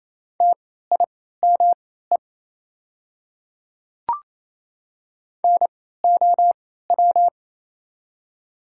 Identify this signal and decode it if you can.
Morse code, heard as a keyed tone, about 27 wpm, spaced as for 16 wpm, 700 Hz. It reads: TIME NOW
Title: 3x Character Spacing - Top 200 Words - 28wpm